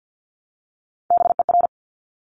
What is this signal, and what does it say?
Morse code, heard as a keyed tone, about 50 wpm, 700 Hz. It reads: BER
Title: Top Three Letter Combinations in Words 50wpm